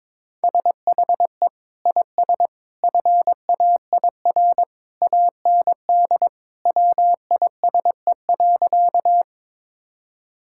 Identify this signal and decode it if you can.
Morse code, heard as a keyed tone, about 22 wpm, 700 Hz. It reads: SHE IS FAIR AND WISE.
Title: Sentences from Top 1600 Words 22wpm